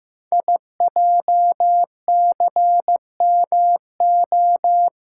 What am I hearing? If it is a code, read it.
Morse code, heard as a keyed tone, about 15 wpm, 700 Hz. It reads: IJCMO
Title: Code Groups - 5 Letters - 15wpm